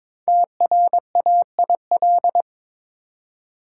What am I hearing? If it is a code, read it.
Morse code, heard as a keyed tone, about 22 wpm, 700 Hz. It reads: TRAIL
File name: 5 Letter Words - First 3 Letters Given - 22wpm